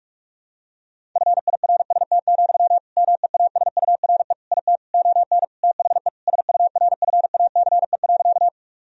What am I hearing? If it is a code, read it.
Morse code, heard as a keyed tone, about 46 wpm, 700 Hz. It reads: FIRST, MEASURE IT ON THE SURFACE.
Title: Sentences from Top 1000 Words - Review Part 3 - 45wpm